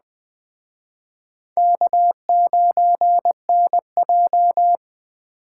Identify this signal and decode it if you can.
Morse code, heard as a keyed tone, about 21 wpm, 700 Hz. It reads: K9NJ
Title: Speed Racing - USA 1x2 Call Signs 17wpm